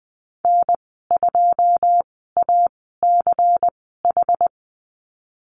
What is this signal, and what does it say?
Morse code, heard as a keyed tone, about 20 wpm, spaced as for 15 wpm, 700 Hz. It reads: N2ACH
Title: 140 Lesson 23g - Letter f - Call Signs 20wpm